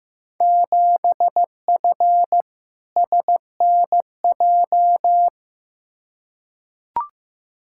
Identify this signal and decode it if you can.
Morse code, heard as a keyed tone, about 15 wpm, 700 Hz. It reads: 7F SNJ
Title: Field Day Exchanges Encore - 15wpm